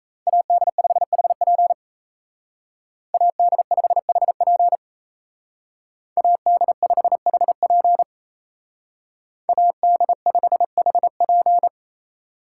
Morse code, heard as a keyed tone, about 36 wpm, 700 Hz. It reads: AD5HP AD5HP AD5HP AD5HP